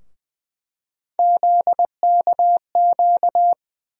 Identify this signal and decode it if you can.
Morse code, heard as a keyed tone, about 21 wpm, 700 Hz. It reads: ZKQ
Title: USA Call Sign Postfixes - 3 Letters 20wpm